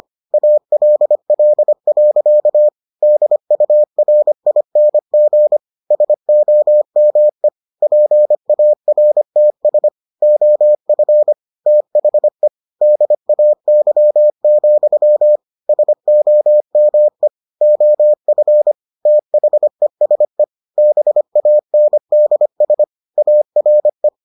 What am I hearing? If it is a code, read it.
Morse code, heard as a keyed tone, about 25 wpm, 600 Hz. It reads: ALL. DURING SOME PARTS OF THE DAY, SOME OF THESE BANDS ARE